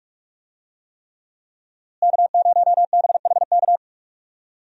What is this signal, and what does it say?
Morse code, heard as a keyed tone, about 45 wpm, 700 Hz. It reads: K0BHK
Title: USA Call Signs - 5 Characters - Repeated 3x - 45wpm